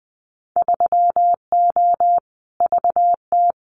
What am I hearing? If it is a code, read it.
Morse code, heard as a keyed tone, about 20 wpm, 700 Hz. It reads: 3O VT
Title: Winter Field Day Exchanges Encore 20wpm